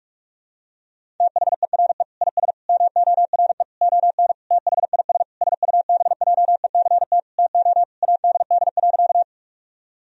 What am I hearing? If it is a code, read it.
Morse code, heard as a keyed tone, about 46 wpm, 700 Hz. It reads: THERE IS MORE ON THIS SUBJECT TO ADD.
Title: Sentences from Top 1000 Words - Review Part 1 - 45wpm